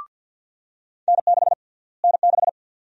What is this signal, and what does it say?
Morse code, heard as a keyed tone, about 53 wpm, 700 Hz. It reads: N6 N6
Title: USA Call Sign Prefixes - Repeated 2x - 50wpm